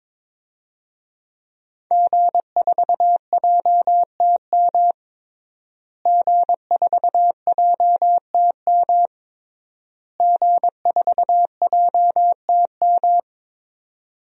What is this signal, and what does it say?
Morse code, heard as a keyed tone, about 22 wpm, 700 Hz. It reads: G4JTM G4JTM G4JTM
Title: DXCC-100 Call Signs - 5 Characters - Repeated 3x - 22wpm